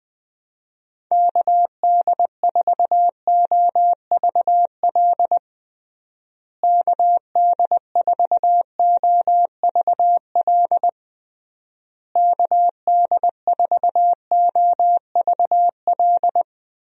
Morse code, heard as a keyed tone, about 21 wpm, 700 Hz. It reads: KD4OVL KD4OVL KD4OVL
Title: USA Call Signs - 6 Characters - Repeated 3x - 20wpm